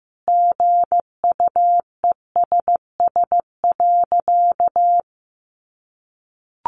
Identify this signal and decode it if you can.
Morse code, heard as a keyed tone, about 15 wpm, 700 Hz. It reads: GUESS.
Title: Sentences from Top 1500 Words 15wpm